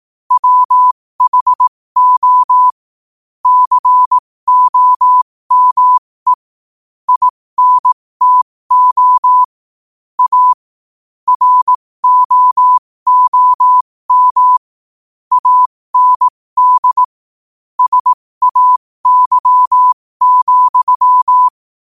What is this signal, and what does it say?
Morse code, heard as a keyed tone, about 18 wpm, spaced as for 15 wpm, 1000 Hz. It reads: WHO COME INTO A ROOM AND SAY,
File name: qotdfiles / 15wpm